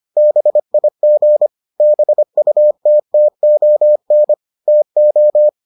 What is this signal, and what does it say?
Morse code, heard as a keyed tone, about 25 wpm, 600 Hz. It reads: BIG BUTTON TO